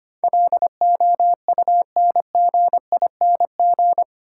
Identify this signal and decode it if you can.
Morse code, heard as a keyed tone, about 25 wpm, 700 Hz. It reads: LOUNGING
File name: Easy Words - Speed-Racing with Hints - 25wpm